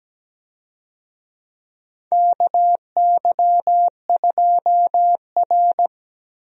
Morse code, heard as a keyed tone, about 17 wpm, 700 Hz. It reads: KY2R
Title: USA Call Signs - 4 Characters - Repeated 3x - 17wpm